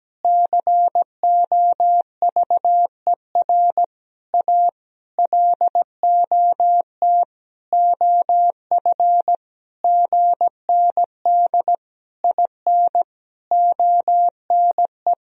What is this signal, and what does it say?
Morse code, heard as a keyed tone, about 17 wpm, 700 Hz. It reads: COVER A LOT OF GND IN ONE
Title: Sentences from Top 500 Words and Abbreviations 17wpm